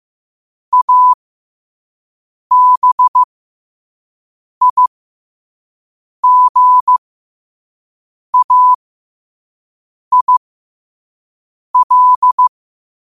Morse code, heard as a keyed tone, about 15 wpm, spaced as for 5 wpm, 1000 Hz. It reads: ABIGAIL
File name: qotdfiles / 5wpm